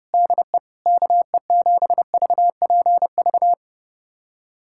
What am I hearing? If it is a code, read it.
Morse code, heard as a keyed tone, about 32 wpm, 700 Hz. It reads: DE KE7VPV
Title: USA Callsign DE Callsign - 6 Characters - 30wpm